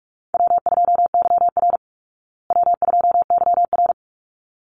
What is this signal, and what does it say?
Morse code, heard as a keyed tone, about 44 wpm, 700 Hz. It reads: W2YR W2YR
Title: Easy Call Signs Repeated 2x - 45wpm